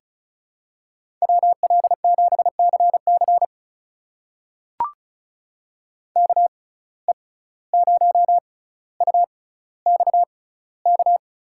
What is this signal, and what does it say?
Morse code, heard as a keyed tone, about 36 wpm, spaced as for 17 wpm, 700 Hz. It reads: WL7CC KE0UXK
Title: USA Call Signs - 6x Character Spacing - 35wpm